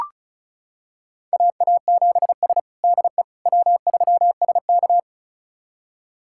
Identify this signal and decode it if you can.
Morse code, heard as a keyed tone, about 36 wpm, 700 Hz. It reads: AA7S DE W3SK
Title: USA Callsign DE Callsign - 4 Characters - 35wpm